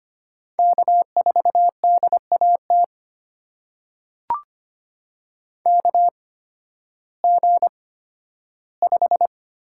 Morse code, heard as a keyed tone, about 25 wpm, spaced as for 13 wpm, 700 Hz. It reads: K4DAT KG5
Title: USA Call Signs - 8x Character Spacing - 25wpm